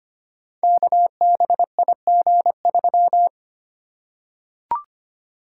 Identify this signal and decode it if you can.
Morse code, heard as a keyed tone, about 25 wpm, 700 Hz. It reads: KBIG3
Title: Code Groups - 5 Letter:Numbers - 25wpm